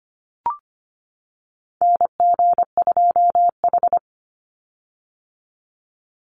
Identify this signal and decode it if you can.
Morse code, heard as a keyed tone, about 25 wpm, 700 Hz. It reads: NG2H
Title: Call Signs - 25wpm